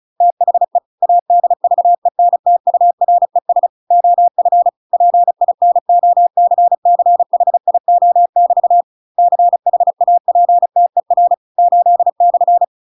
Morse code, heard as a keyed tone, about 34 wpm, 700 Hz. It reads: THE ADVENTURES OF PINOCCHIO<BT> CHAPTER 8/
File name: Pinocchio / Pinocchio ch8 35 35 ik1hge